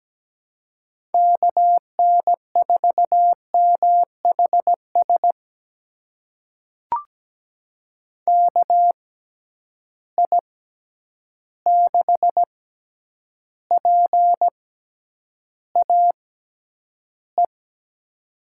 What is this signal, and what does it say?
Morse code, heard as a keyed tone, about 17 wpm, spaced as for 9 wpm, 700 Hz. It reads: KN4MHS KI6PAE
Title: USA Call Signs - 6 Characters - 6x Character Spacing - 17wpm